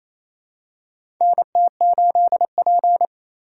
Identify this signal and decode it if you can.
Morse code, heard as a keyed tone, about 28 wpm, 700 Hz. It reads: NT8P
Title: USA Call Signs - 8x Character Spacing - 28wpm